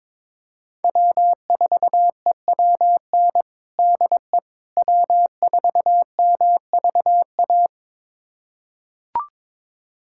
Morse code, heard as a keyed tone, about 22 wpm, 700 Hz. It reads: W4EWN DE W4MVA
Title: USA Callsign DE Callsign - 5 Characters - 22wpm